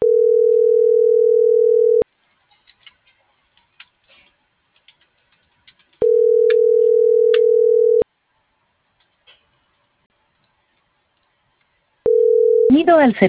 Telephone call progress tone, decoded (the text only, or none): ringback tone